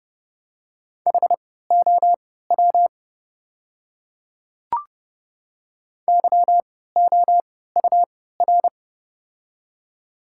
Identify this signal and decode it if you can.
Morse code, heard as a keyed tone, about 30 wpm, spaced as for 17 wpm, 700 Hz. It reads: HOW YOUR
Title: 3x Character Spacing - Top 100 Words - 30wpm